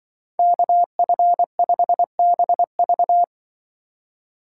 Morse code, heard as a keyed tone, about 24 wpm, 700 Hz. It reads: KF5BV
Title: Speed Racing - USA 2x2 Call Signs 20wpm